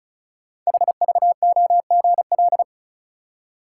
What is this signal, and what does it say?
Morse code, heard as a keyed tone, about 36 wpm, 690 Hz. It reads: HVOGL
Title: Code Groups - 5 Letter:Numbers - 35wpm